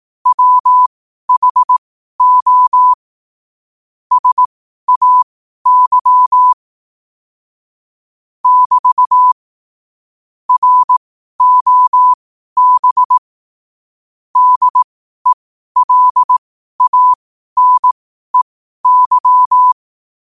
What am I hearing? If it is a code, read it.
Morse code, heard as a keyed tone, about 18 wpm, spaced as for 12 wpm, 1000 Hz. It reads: WHO SAY <BT> ROB DELANEY
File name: qotdfiles / 12wpm